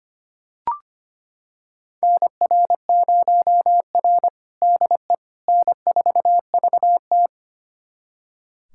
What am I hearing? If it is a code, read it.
Morse code, heard as a keyed tone, about 25 wpm, 700 Hz. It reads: NR0R DE N4VT
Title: USA Callsign DE Callsign - 4 Characters - 25wpm